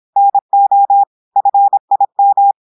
Morse code, heard as a keyed tone, about 26 wpm, 800 Hz. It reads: NO FIM
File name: portugais texte en clair 2 vitesse 26 mots minutes